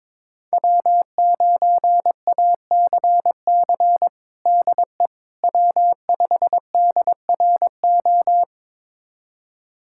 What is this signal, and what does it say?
Morse code, heard as a keyed tone, about 22 wpm, 700 Hz. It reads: W9ACC DE W5DRO
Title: USA Callsign DE Callsign - 5 Characters - 22wpm